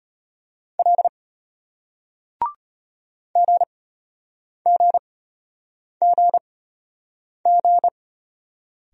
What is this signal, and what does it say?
Morse code, heard as a keyed tone, about 34 wpm, spaced as for 8 wpm, 700 Hz. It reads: L GGGG